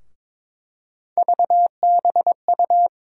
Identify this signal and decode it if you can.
Morse code, heard as a keyed tone, about 23 wpm, 700 Hz. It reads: VBU
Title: USA Call Sign Postfixes - 3 Letters 22wpm